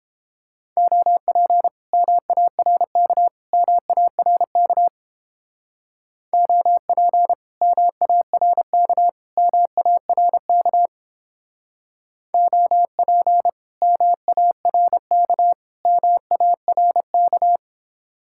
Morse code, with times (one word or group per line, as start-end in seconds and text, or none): 0.8-1.7 OP
1.9-3.3 MARK
3.5-4.9 MARK
6.3-7.3 OP
7.6-9.1 MARK
9.4-10.9 MARK
12.3-13.5 OP
13.8-15.5 MARK
15.9-17.6 MARK